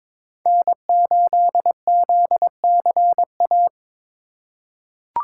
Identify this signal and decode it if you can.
Morse code, heard as a keyed tone, about 22 wpm, 700 Hz. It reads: N8ZCA
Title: USA Call Signs - 5 Characters - 2x Character Spacing - 22wpm